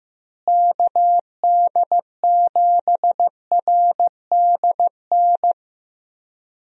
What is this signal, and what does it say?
Morse code, heard as a keyed tone, about 15 wpm, 700 Hz. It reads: KD7RDN
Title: USA Call Signs - 6 Characters - Repeated 3x - 15wpm